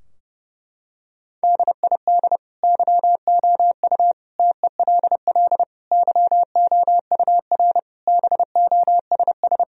Morse code, heard as a keyed tone, about 30 wpm, 700 Hz. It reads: DID YOU TELL YOUR BOSS
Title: Sentences from Top 1000 Words - Review Part 4 - 30wpm